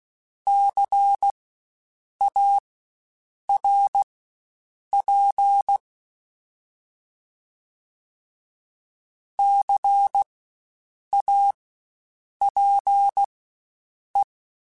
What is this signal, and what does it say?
Morse code, heard as a keyed tone, about 16 wpm, spaced as for 7 wpm, 800 Hz. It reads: CARP CAPE
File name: Spaced Alphas / Spaced Alphas mp3s / Cs sp @15